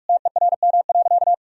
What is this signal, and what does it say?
Morse code, heard as a keyed tone, about 46 wpm, 700 Hz. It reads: TERM.
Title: Sentences from Top 1000 Words - Review Part 4 - 45wpm